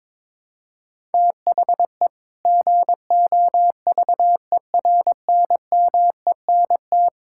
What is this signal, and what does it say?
Morse code, heard as a keyed tone, about 22 wpm, 700 Hz. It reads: THE GOVERNMENT
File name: practice / 22wpm / Sentences with 2-Word Phrases - 22wpm